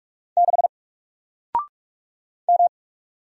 Morse code, heard as a keyed tone, about 45 wpm, 700 Hz. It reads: 6 M
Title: Single Character (Letter:Number:Punctuation:Prosign) Alt 45wpm